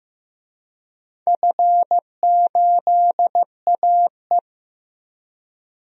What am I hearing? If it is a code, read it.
Morse code, heard as a keyed tone, about 15 wpm, 700 Hz. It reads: F8AE
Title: DXCC-100 Call Signs - 4 Characters - Repeated 2x - 15wpm